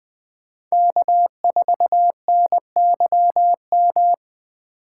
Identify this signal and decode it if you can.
Morse code, heard as a keyed tone, about 20 wpm, 700 Hz. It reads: K4NYM
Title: 2023 POTA Activator Call Signs - 20wpm